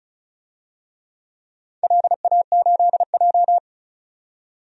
Morse code, heard as a keyed tone, about 34 wpm, 700 Hz. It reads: LA8J
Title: DXCC-100 Call Signs - 4 Characters - 35wpm